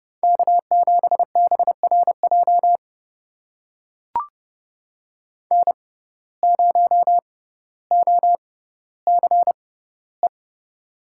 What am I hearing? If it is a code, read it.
Morse code, heard as a keyed tone, about 32 wpm, spaced as for 16 wpm, 700 Hz. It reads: K7BRJ N0OCE
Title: USA Call Signs - 5 Characters - 6x Character Spacing - 30wpm